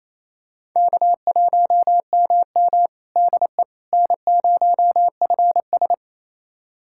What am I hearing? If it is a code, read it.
Morse code, heard as a keyed tone, about 27 wpm, 700 Hz. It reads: K1MM DE N0FS
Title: USA Callsign DE Callsign - 4 Characters - 28wpm